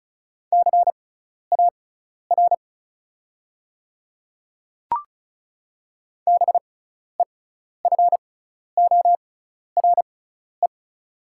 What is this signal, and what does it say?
Morse code, heard as a keyed tone, about 36 wpm, spaced as for 12 wpm, 690 Hz. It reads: CAR BEFORE
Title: 6x Character Spacing - Top 300 Words - 35wpm